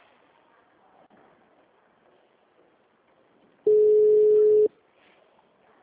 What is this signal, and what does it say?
Ringback tone (Europe, 425 Hz).